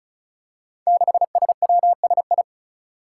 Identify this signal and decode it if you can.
Morse code, heard as a keyed tone, about 36 wpm, 700 Hz. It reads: 6SWSI
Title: Five Challenging Letter-Number Combinations 35wpm